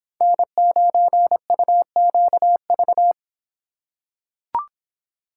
Morse code, heard as a keyed tone, about 26 wpm, 700 Hz. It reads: N9UQV